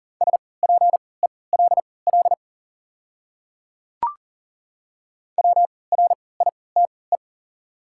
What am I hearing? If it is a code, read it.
Morse code, heard as a keyed tone, about 40 wpm, spaced as for 23 wpm, 690 Hz. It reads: SPELL WRITE
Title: 3x Character Spacing - Top 300 Words - 40wpm